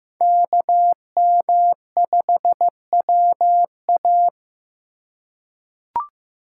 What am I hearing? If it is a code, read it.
Morse code, heard as a keyed tone, about 15 wpm, 700 Hz. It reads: KM5WA
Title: USA Call Signs - 5 Characters - 4x Character Spacing - 15wpm